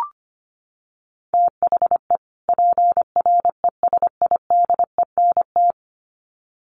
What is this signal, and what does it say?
Morse code, heard as a keyed tone, about 25 wpm, 700 Hz. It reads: THE PRESIDENT
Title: Sets of 2 Words 25wpm